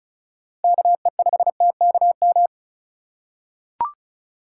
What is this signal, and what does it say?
Morse code, heard as a keyed tone, about 34 wpm, 700 Hz. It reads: KE5TKM
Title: USA Call Signs - 6 Characters - 2x Character Spacing - 35wpm